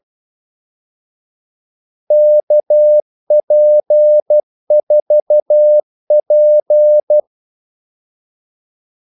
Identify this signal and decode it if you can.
Morse code, heard as a keyed tone, about 12 wpm, 600 Hz. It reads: KP4P
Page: 2 ekstra ordmellemrum mellem kaldesignalerne